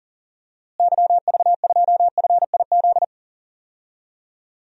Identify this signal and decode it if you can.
Morse code, heard as a keyed tone, about 40 wpm, 700 Hz. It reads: YV2FIZ
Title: DXCC-100 Call Signs - 6 Characters - Repeated 2x - 40wpm